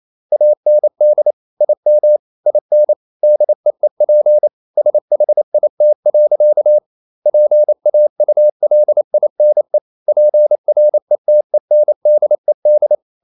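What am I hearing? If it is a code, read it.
Morse code, heard as a keyed tone, about 28 wpm, 600 Hz. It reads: AND IM IN DEEP SHIT. PAULINE PRETENDED
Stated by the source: Morse code practice at 28 wpm, text of Ken Follett's novel Never in English, from CW med Gnister.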